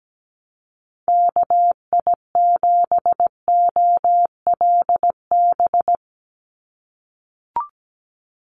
Morse code, heard as a keyed tone, about 17 wpm, 700 Hz. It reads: KI7OLB
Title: USA Call Signs 17wpm